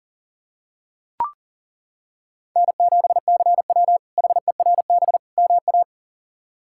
Morse code, heard as a keyed tone, about 37 wpm, 700 Hz. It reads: N7CW HERB MA
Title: K1USN SST QSO Practice - 40wpm